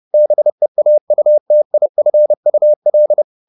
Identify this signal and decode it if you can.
Morse code, heard as a keyed tone, about 30 wpm, 600 Hz. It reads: BEAUTIFUL